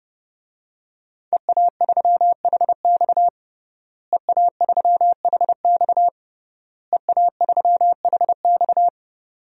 Morse code, heard as a keyed tone, about 30 wpm, 700 Hz. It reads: EA3HX EA3HX EA3HX
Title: DXCC-100 Call Signs - 5 Characters - Repeated 3x - 30wpm